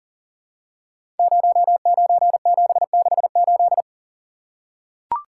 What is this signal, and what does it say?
Morse code, heard as a keyed tone, about 39 wpm, 700 Hz. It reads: 09768
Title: Code Groups - 5 Letter:Numbers - 40wpm